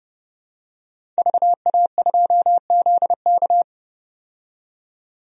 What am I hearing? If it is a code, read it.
Morse code, heard as a keyed tone, about 30 wpm, 700 Hz. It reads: VA2ZK
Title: DXCC-100 Call Signs - 5 Characters - 30wpm